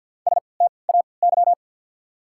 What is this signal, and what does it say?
Morse code, heard as a keyed tone, about 53 wpm, spaced as for 29 wpm, 700 Hz. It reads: STAY
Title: 3x Character Spacing - Top 500 Words - 50wpm